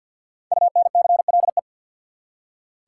Morse code, heard as a keyed tone, about 47 wpm, 700 Hz. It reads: UNCLE
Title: Five Letter Words - 50wpm